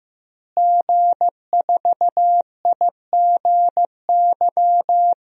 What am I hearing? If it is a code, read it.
Morse code, heard as a keyed tone, about 15 wpm, 700 Hz. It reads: G4IGY
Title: DXCC-100 Call Signs - 5 Characters - Repeated 2x - 15wpm